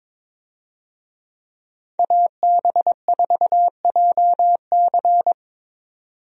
Morse code, heard as a keyed tone, about 21 wpm, 700 Hz. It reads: AB4JC
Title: USA Call Signs - 5 Characters - 22wpm